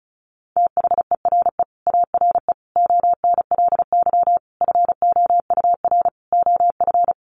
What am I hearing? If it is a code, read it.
Morse code, heard as a keyed tone, about 34 wpm, 700 Hz. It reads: THERE ARE ONLY FOUR OF
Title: Sentences from Top 300 Words 35wpm